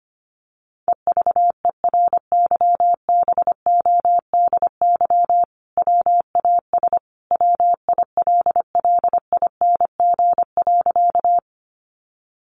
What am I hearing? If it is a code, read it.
Morse code, heard as a keyed tone, about 25 wpm, 700 Hz. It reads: EVERYBODY WAS WILLING.
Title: Sentences from Top 1400 Words 25wpm